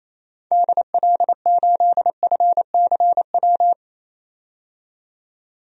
Morse code, heard as a keyed tone, about 28 wpm, 700 Hz. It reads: DL8FCW
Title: DXCC-100 Call Signs - 6 Characters - 28wpm